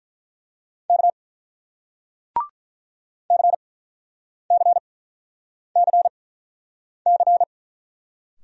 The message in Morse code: K CCCC

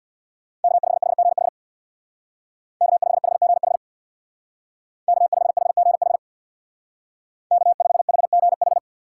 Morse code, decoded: K5HGH K5HGH K5HGH K5HGH

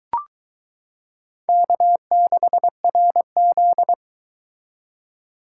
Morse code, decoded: K6RZ

